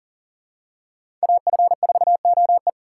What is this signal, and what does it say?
Morse code, heard as a keyed tone, about 39 wpm, 700 Hz. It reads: AF4OE
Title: USA Call Signs - 5 Characters - 3x Character Spacing - 40wpm